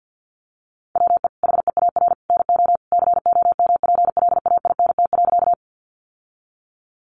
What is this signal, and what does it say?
Morse code, heard as a keyed tone, about 50 wpm, 700 Hz. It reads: WE HEAR NO COMPLAINT.
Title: Sentences from Top 1400 Words 50wpm